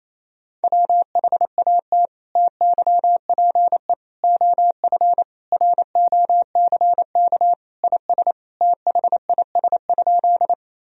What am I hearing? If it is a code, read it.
Morse code, heard as a keyed tone, about 28 wpm, 700 Hz. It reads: WHAT TYPE OF ROCK IS THIS?
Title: Sentences from Top 1000 Words - Review Part 3 - 28wpm